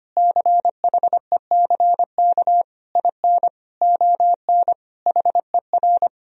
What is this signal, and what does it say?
Morse code, heard as a keyed tone, about 25 wpm, 700 Hz. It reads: CHECK IN ON HER
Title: Sentences from Top 1000 Words - Review Part 3 - 25wpm